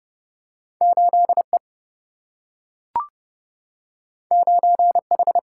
Morse code, heard as a keyed tone, about 30 wpm, 700 Hz. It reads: 8 E 9 H
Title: Single-Letter Single-Number 30wpm